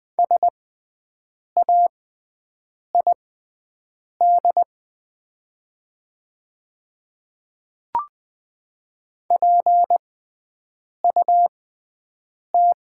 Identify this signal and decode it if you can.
Morse code, heard as a keyed tone, about 20 wpm, spaced as for 7 wpm, 700 Hz. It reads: SAID PUT
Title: 6x Character Spacing - Top 200 Words - 20wpm